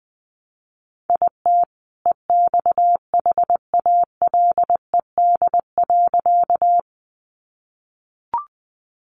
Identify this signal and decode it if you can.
Morse code, heard as a keyed tone, about 20 wpm, 700 Hz. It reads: IT EXHALED.
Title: A Sound Of Thunder 20wpm